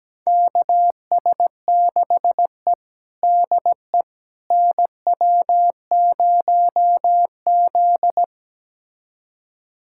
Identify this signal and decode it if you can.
Morse code, heard as a keyed tone, about 17 wpm, 700 Hz. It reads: KS6E DE NW0Z
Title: USA Callsign DE Callsign - 4 Characters - 17wpm